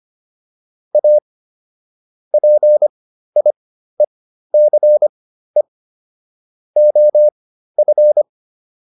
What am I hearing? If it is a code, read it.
Morse code, heard as a keyed tone, about 25 wpm, spaced as for 13 wpm, 600 Hz. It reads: A PIECE OF